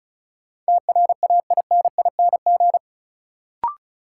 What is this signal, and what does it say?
Morse code, heard as a keyed tone, about 36 wpm, 700 Hz. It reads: TRAINING
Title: Sentences from Top 1000 Words - Review Part 4 - New Word Drill - 35wpm